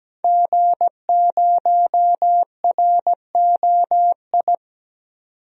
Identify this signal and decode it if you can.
Morse code, heard as a keyed tone, about 17 wpm, 700 Hz. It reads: G0ROI
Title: DXCC-100 Call Signs - 5 Characters - 17wpm